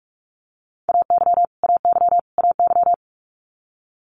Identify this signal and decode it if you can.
Morse code, heard as a keyed tone, about 46 wpm, 700 Hz. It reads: AY AY AY
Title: Multi-letter Phonemes Repeated 3x 45wpm